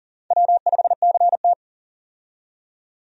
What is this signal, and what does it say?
Morse code, heard as a keyed tone, about 40 wpm, 700 Hz. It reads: W5CT
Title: USA Call Signs - 4 Characters - 40wpm